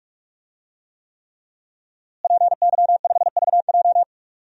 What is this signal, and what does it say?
Morse code, heard as a keyed tone, about 43 wpm, 700 Hz. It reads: PY5VJ
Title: DXCC-100 Call Signs - 5 Characters - Repeated 3x - 45wpm